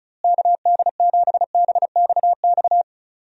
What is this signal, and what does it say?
Morse code, heard as a keyed tone, about 36 wpm, 700 Hz. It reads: KD7BXX
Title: USA Call Signs - 6 Characters - Repeated 3x - 35wpm